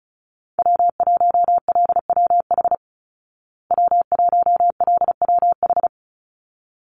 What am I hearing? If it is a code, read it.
Morse code, heard as a keyed tone, about 34 wpm, 700 Hz. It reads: W1LWH W1LWH
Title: Call Signs Repeated 3x - 35wpm